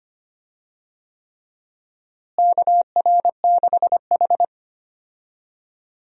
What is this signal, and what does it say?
Morse code, heard as a keyed tone, about 25 wpm, 700 Hz. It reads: KR6H
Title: USA Call Signs - 4 Characters - Repeated 3x - 25wpm